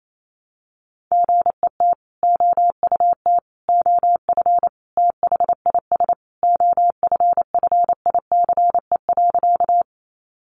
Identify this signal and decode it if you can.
Morse code, heard as a keyed tone, about 28 wpm, 700 Hz. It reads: GET OUT OF THIS OFFICE.
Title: Sentences from Top 1000 Words 28wpm